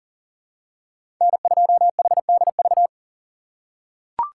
Morse code, heard as a keyed tone, about 37 wpm, 700 Hz. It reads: N2HDV
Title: USA Call Signs - 5 Characters - 6x Character Spacing - 40wpm